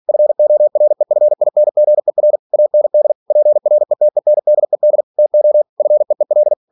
Morse code, heard as a keyed tone, about 46 wpm, 600 Hz. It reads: FOREFINGER AND PRETENDED TO FEEL
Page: Never 47wpm